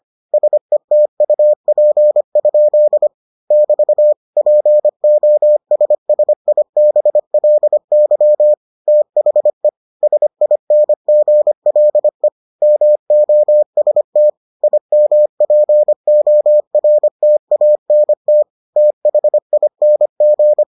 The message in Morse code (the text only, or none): SETUP? <BT> POSSIBLY THE SINGLE MOST IMPORTANT THING